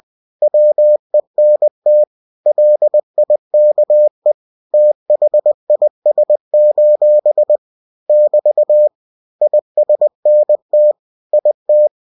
Morse code, heard as a keyed tone, about 20 wpm, 600 Hz. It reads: WENT LIKE THIS: <BT> ISNT IT
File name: cw-books / winnie-the-pooh / winnie-20wpm-page-0003